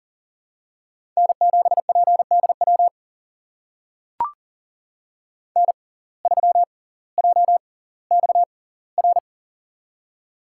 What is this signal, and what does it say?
Morse code, heard as a keyed tone, about 40 wpm, spaced as for 20 wpm, 700 Hz. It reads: N7PDW N3JXR
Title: USA Call Signs - 5 Characters - 6x Character Spacing - 40wpm